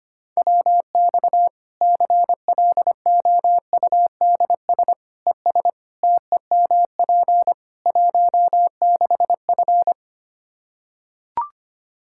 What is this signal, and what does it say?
Morse code, heard as a keyed tone, about 25 wpm, 700 Hz. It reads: WX CLOUDS ES TEMP 16F
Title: Practice QSO - Exchange 2 - Info - 25wpm